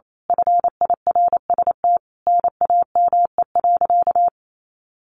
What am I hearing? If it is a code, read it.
Morse code, heard as a keyed tone, about 28 wpm, 700 Hz. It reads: FIRST NAME.
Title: Sentences from Top 1200 Words 28wpm